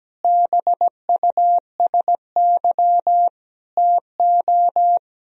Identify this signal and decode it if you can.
Morse code, heard as a keyed tone, about 17 wpm, 700 Hz. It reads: BUSY TO
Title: Sentences from Top 1000 Words - Review Part 3 - 17wpm